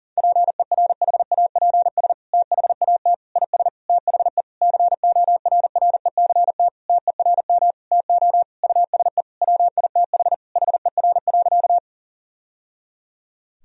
Morse code, heard as a keyed tone, about 40 wpm, 700 Hz. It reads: PERHAPS THAT IS THE CORRECT TERM TO USE WITH HER.
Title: Sentences from Top 1000 Words - Review Part 4 - 40wpm